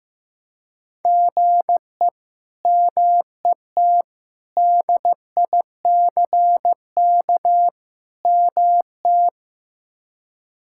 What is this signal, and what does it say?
Morse code, heard as a keyed tone, about 15 wpm, 700 Hz. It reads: GE MET DICK MT
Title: K1USN SST QSO Practice - 15wpm